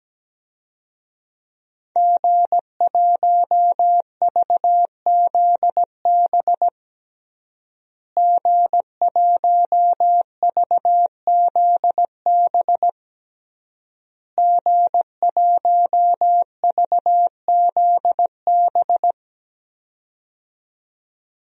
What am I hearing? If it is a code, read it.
Morse code, heard as a keyed tone, about 17 wpm, 700 Hz. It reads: G1VZB G1VZB G1VZB
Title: DXCC-100 Call Signs - 5 Characters - Repeated 3x - 17wpm